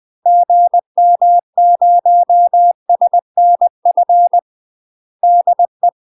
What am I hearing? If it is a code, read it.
Morse code, heard as a keyed tone, about 20 wpm, 700 Hz. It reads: GM0SNF DE
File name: UK MASTER QSOs TRAINING / 20WPM / UK MASTER QSOs 0023